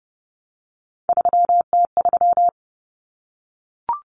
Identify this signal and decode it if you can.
Morse code, heard as a keyed tone, about 30 wpm, 700 Hz. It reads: 3T3
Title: Common Cut Numbers - 3-Digit 30wpm